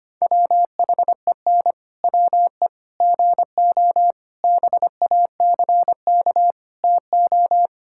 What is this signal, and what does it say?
Morse code, heard as a keyed tone, about 25 wpm, 700 Hz. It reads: WHEN WE GO BACK TO
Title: Pangram Challenge v2 - 25wpm